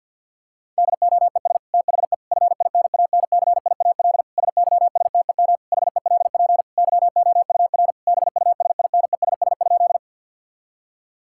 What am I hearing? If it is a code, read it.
Morse code, heard as a keyed tone, about 50 wpm, 700 Hz. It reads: DOES THE FINANCIAL SYSTEM HELP YOUR BUSINESS?
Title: Sentences from Top 1000 Words - Review Part 5 - 50wpm